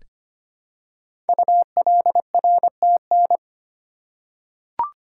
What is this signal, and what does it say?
Morse code, heard as a keyed tone, about 25 wpm, 700 Hz. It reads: ULRTN
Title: Code Groups - 5 Letters - 25wpm